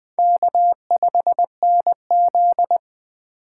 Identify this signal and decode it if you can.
Morse code, heard as a keyed tone, about 20 wpm, 700 Hz. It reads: K5NZ